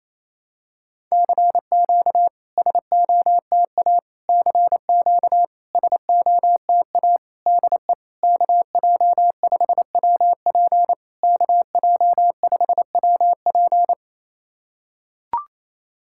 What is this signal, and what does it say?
Morse code, heard as a keyed tone, about 27 wpm, 690 Hz. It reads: CQ SOTA CQ SOTA DE KJ5WP KJ5WP
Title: SOTA QSO Practice 28wpm